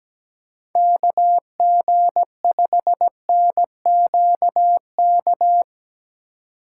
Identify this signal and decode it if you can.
Morse code, heard as a keyed tone, about 17 wpm, 700 Hz. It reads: KG5NQK
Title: USA Call Signs - 6 Characters - 3x Character Spacing - 17wpm